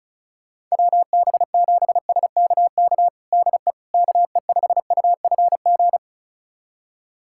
Morse code, heard as a keyed tone, about 34 wpm, 700 Hz. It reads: WB7SKK DE KE5UFG
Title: USA Callsign DE Callsign - 6 Characters - 35wpm